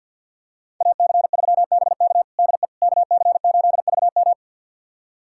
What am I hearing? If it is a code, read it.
Morse code, heard as a keyed tone, about 49 wpm, 700 Hz. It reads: AC3BK DE KC8VM
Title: USA Callsign DE Callsign - 5 Characters - 50wpm